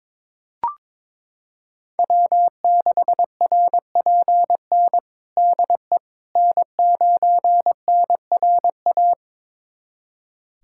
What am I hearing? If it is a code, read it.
Morse code, heard as a keyed tone, about 22 wpm, 700 Hz. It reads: W6RPN DE N9NRA
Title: USA Callsign DE Callsign - 5 Characters - 22wpm